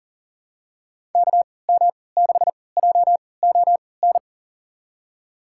Morse code, heard as a keyed tone, about 40 wpm, spaced as for 23 wpm, 700 Hz. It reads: KM6JON
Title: USA Call Signs - 6 Characters - 3x Character Spacing - 40wpm